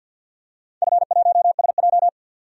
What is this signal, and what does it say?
Morse code, heard as a keyed tone, about 48 wpm, 700 Hz. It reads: F1SJ
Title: DXCC-100 Call Signs - 4 Characters - 50wpm